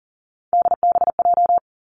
Morse code, heard as a keyed tone, about 40 wpm, 700 Hz. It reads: DBJ
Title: Three Challenging Letter-Number Combinations 40wpm